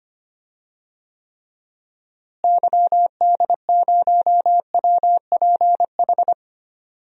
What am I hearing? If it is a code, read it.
Morse code, heard as a keyed tone, about 25 wpm, 700 Hz. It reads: YD0WPH
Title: DXCC-100 Call Signs - 6 Characters - Repeated 3x - 25wpm